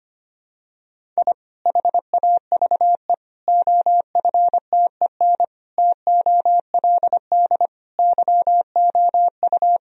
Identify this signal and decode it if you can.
Morse code, heard as a keyed tone, about 25 wpm, 700 Hz. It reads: I HAVE OFTEN TOLD YOU
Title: Sentences from Top 1000 Words - Review Part 1 - 25wpm